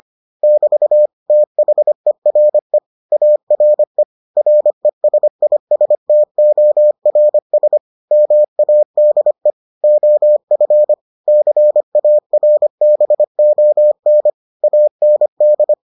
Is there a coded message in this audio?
Yes: Morse code from 0.4 to 15.7 s, heard as a keyed tone, about 25 wpm, 600 Hz. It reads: <BT> THERE ARE RESISTORS MADE OF CARBON AND